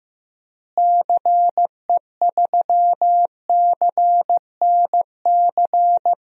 Morse code, heard as a keyed tone, about 15 wpm, 700 Hz. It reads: CE3CNC
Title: DXCC-100 Call Signs - 6 Characters - Repeated 3x - 15wpm